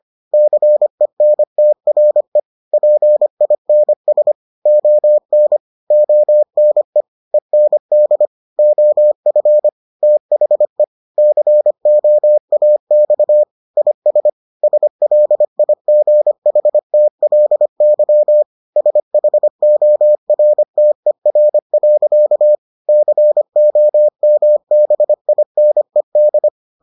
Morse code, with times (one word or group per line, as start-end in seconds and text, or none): 0.3-2.4 CENTRE
2.7-4.3 PINS
4.7-5.6 ON
5.9-7.0 ONE
7.3-8.3 END
8.6-9.7 OF
10.0-10.8 THE
11.2-13.4 COAX
13.8-14.3 IS
14.6-18.4 SLIGHTLY
18.8-22.6 SHORTER.
22.9-26.5 COMBINED